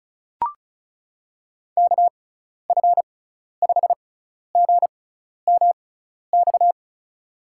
Morse code, heard as a keyed tone, about 36 wpm, spaced as for 12 wpm, 700 Hz. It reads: KF5GMX
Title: USA Call Signs - 6 Characters - 6x Character Spacing - 35wpm